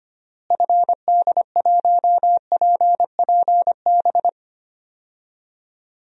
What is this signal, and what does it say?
Morse code, heard as a keyed tone, about 25 wpm, 700 Hz. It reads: FD1PPB
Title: DXCC-100 Call Signs - 6 Characters - Repeated 3x - 25wpm